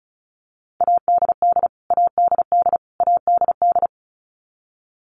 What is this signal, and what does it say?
Morse code, heard as a keyed tone, about 34 wpm, 700 Hz. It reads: ADD ADD ADD
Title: Top 200 Words Repeated 35wpm